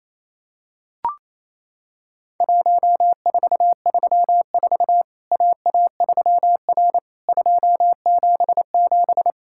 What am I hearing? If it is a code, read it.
Morse code, heard as a keyed tone, about 27 wpm, 700 Hz. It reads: 1434 AA3R 277
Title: Worked All Europe DX Contest - QTC Exchange - 28wpm